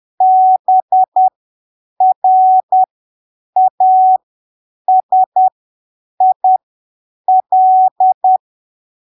Morse code, heard as a keyed tone, about 10 wpm, spaced as for 7 wpm, 750 Hz. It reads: BRASIL
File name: arquivos / texto / texto 14 10ppm